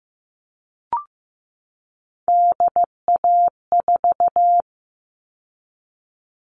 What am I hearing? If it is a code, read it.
Morse code, heard as a keyed tone, about 15 wpm, 700 Hz. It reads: DA4
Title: All Cut Numbers - 3-Digit 15wpm